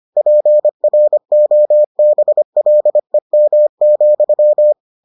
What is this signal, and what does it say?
Morse code, heard as a keyed tone, about 25 wpm, 600 Hz. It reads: PROBLEM,